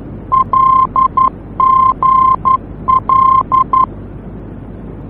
Morse code, heard as a keyed tone, about 11 wpm, 1050 Hz. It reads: LGL
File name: france / mx / LGL-F-VOR